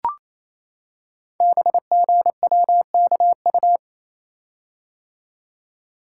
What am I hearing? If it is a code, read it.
Morse code, heard as a keyed tone, about 27 wpm, 700 Hz. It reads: BGWKU
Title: Code Groups - 5 Letters - 28wpm